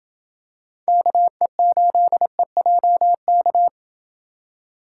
Morse code, heard as a keyed tone, about 26 wpm, 700 Hz. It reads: KE8EJK